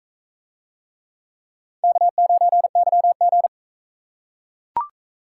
Morse code, heard as a keyed tone, about 43 wpm, 700 Hz. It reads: K9YG